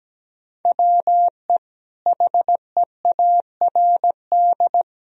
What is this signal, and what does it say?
Morse code, heard as a keyed tone, about 17 wpm, 700 Hz. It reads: WE HEARD
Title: Sentences from Top 1000 Words - Review Part 3 - 17wpm